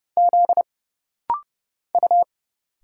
Morse code, heard as a keyed tone, about 31 wpm, 700 Hz. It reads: Z U
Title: Single Letters - Rapid-Fire 30wpm